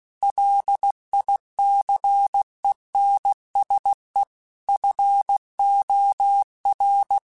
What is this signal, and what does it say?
Morse code, heard as a keyed tone, about 16 wpm, 800 Hz. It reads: LICENSE FOR
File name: QSOs / QSOs-mp3s / kc7haa@15